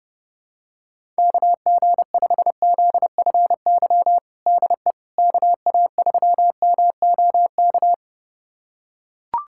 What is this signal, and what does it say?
Morse code, heard as a keyed tone, about 30 wpm, 700 Hz. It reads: KG5ZFY DE KA3MOK
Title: USA Callsign DE Callsign - 6 Characters - 30wpm